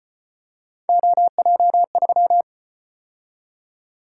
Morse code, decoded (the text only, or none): OJ3